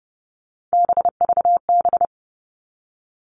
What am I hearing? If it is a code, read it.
Morse code, heard as a keyed tone, about 28 wpm, 700 Hz. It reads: BVB
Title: All Cut Numbers - 3-Digit 30wpm